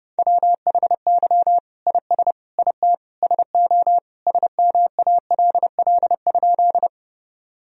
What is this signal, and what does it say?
Morse code, heard as a keyed tone, about 30 wpm, 700 Hz. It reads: WHY IS IT SO SMALL?
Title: Sentences from Top 1000 Words - Review Part 1 - 30wpm